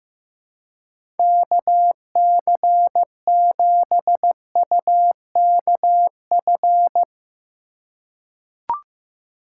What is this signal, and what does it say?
Morse code, heard as a keyed tone, about 15 wpm, 700 Hz. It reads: KC7UKF
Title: USA Call Signs - 6 Characters - 2x Character Spacing - 15wpm